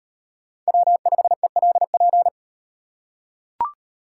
Morse code, heard as a keyed tone, about 38 wpm, 700 Hz. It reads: W5ELP